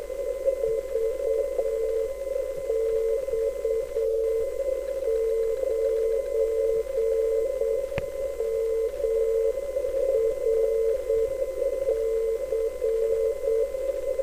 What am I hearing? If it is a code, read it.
Morse code, heard as a keyed tone, about 8 wpm, 450 Hz. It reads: VX9MRC